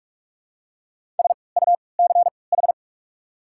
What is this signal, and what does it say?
Morse code, heard as a keyed tone, about 47 wpm, spaced as for 26 wpm, 700 Hz. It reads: SUCH
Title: 3x Character Spacing - Top 300 Words - 45wpm